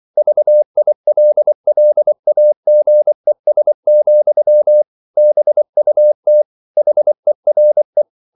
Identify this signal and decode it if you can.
Morse code, heard as a keyed tone, about 24 wpm, 600 Hz. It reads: VILLAGES, BUT HERE